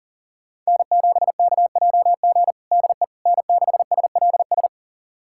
Morse code, heard as a keyed tone, about 40 wpm, 700 Hz. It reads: N7KJG DE N6SLS
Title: USA Callsign DE Callsign - 5 Characters - 40wpm